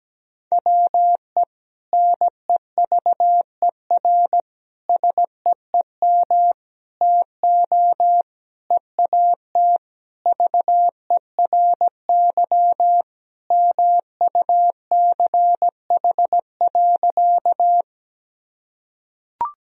Morse code, heard as a keyed tone, about 17 wpm, 700 Hz. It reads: WE NEVER SEEM TO EAT VERY MUCH.
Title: Sentences from Top 1000 Words - Review Part 1 - 17wpm